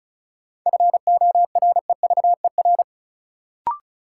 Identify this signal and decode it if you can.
Morse code, heard as a keyed tone, about 36 wpm, 700 Hz. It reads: FOREVER
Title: Sentences from Top 1600 Words - New Word Drill - 35wpm